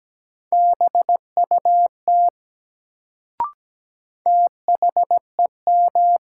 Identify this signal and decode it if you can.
Morse code, heard as a keyed tone, about 17 wpm, 700 Hz. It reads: BUT THEM
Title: Sentences from Top 100 Words - New Word Drill - 17wpm